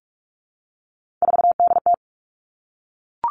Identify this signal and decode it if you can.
Morse code, heard as a keyed tone, about 46 wpm, 700 Hz. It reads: 4DT
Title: Three Challenging Letter-Number Combinations 45wpm